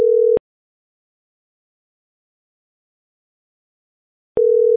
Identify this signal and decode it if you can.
Ringback tone (North America), 440 + 480 Hz.